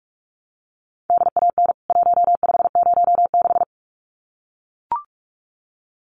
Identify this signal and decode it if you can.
Morse code, heard as a keyed tone, about 44 wpm, 700 Hz. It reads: DAN 1506
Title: CWT Exchange 45wpm